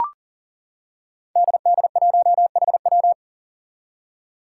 Morse code, heard as a keyed tone, about 37 wpm, 700 Hz. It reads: DD1HW
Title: Five Challenging Letter-Number Combinations 40wpm